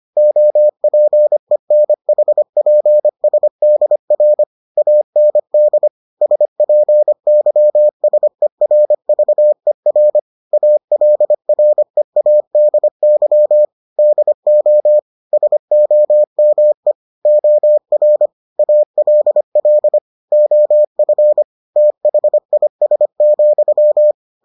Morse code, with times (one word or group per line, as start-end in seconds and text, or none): 0.2-4.4 OPENHPSDR
4.8-5.9 AND
6.2-10.2 SPYSERVER
10.5-13.6 ALREADY
14.0-15.0 DO
15.3-16.9 SOME
17.3-18.3 OR
18.6-20.0 ALL
20.3-21.4 OF
21.8-24.1 THIS,